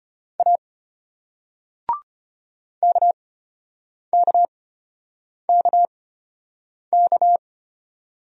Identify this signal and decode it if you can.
Morse code, heard as a keyed tone, about 35 wpm, spaced as for 8 wpm, 700 Hz. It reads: A KKKK